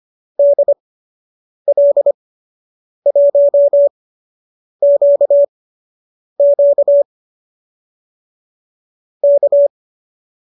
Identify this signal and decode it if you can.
Morse code, heard as a keyed tone, about 25 wpm, spaced as for 8 wpm, 570 Hz. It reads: DL1QQ K